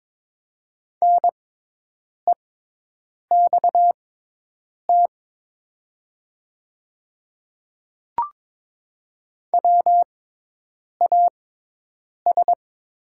Morse code, heard as a keyed tone, about 22 wpm, spaced as for 8 wpm, 700 Hz. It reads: NEXT WAS
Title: 6x Character Spacing - Top 300 Words - 22wpm